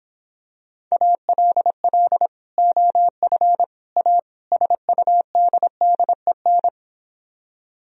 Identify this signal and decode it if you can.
Morse code, heard as a keyed tone, about 26 wpm, 700 Hz. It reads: ALL OF A SUDDEN